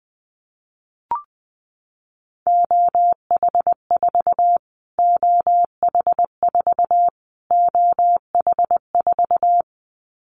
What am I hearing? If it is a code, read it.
Morse code, heard as a keyed tone, about 19 wpm, 700 Hz. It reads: OH4 OH4 OH4
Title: Callsign Prefixes for Top 25 DXCC - Repeated 3x - 20wpm